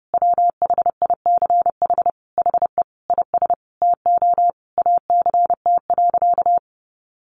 Morse code, heard as a keyed tone, about 30 wpm, 700 Hz. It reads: WHICH HE IS TO ACT.
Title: Sentences from Top 1200 Words 30wpm